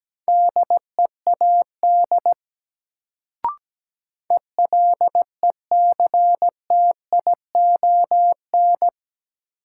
Morse code, heard as a keyed tone, about 17 wpm, 700 Hz. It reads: DEAD ELECTION
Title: Sentences from Top 1500 Words - New Word Drill Alt - 17wpm